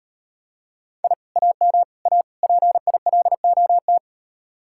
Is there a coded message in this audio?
Yes: Morse code from 1.0 to 4.0 s, heard as a keyed tone, about 39 wpm, 700 Hz. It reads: I AM A PILOT